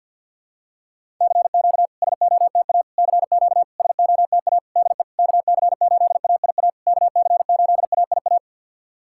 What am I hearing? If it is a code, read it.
Morse code, heard as a keyed tone, about 50 wpm, 700 Hz. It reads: CQ SOTA CQ SOTA DE KC8AIA KC8AIA
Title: SOTA QSO Practice 50wpm